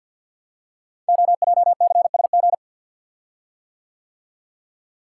Morse code, heard as a keyed tone, about 50 wpm, 690 Hz. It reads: OJCSG
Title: Code Groups - 5 Letters - 50wpm